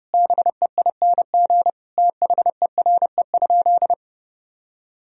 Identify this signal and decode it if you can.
Morse code, heard as a keyed tone, about 32 wpm, 700 Hz. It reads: BEING THERE?
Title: Sentences from Top 1000 Words - Review Part 3 - 30wpm